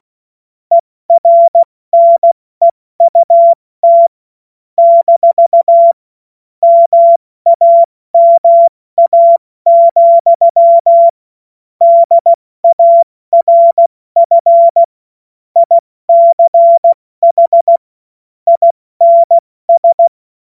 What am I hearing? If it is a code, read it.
Morse code, heard as a keyed tone, about 16 wpm, 680 Hz. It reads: ERNEUT - MAMA, DARF ICH INS
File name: Morsekurs / Witze / Witz 21. September 2021, 70 F80